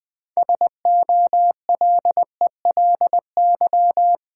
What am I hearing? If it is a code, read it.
Morse code, heard as a keyed tone, about 20 wpm, 700 Hz. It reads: SOLELY
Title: Build A Word - 3 Letters At A Time - 6-Letter Words - 20wpm